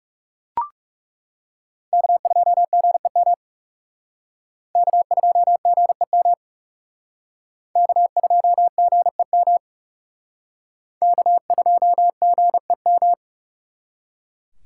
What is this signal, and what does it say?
Morse code, heard as a keyed tone, about 38 wpm, 700 Hz. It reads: K2GEM K2GEM K2GEM K2GEM